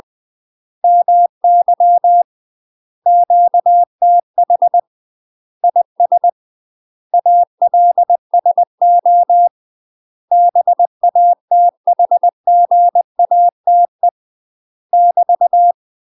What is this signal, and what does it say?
Morse code, heard as a keyed tone, about 20 wpm, 700 Hz. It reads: MY QTH IS ALSO BATHGATE <BT>
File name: UK MASTER QSOs TRAINING/20WPM